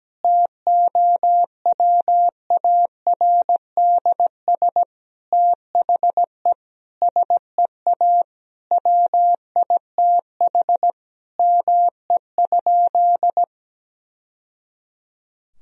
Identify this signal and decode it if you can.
Morse code, heard as a keyed tone, about 17 wpm, 700 Hz. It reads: TOWARDS THE SEA WITH ME?
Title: Sentences from Top 1000 Words - Review Part 2 - 17wpm